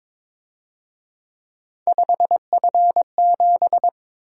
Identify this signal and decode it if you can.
Morse code, heard as a keyed tone, about 26 wpm, 700 Hz. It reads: 5F7
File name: Speed-Racing - Three Letter-Number Combinations - 22wpm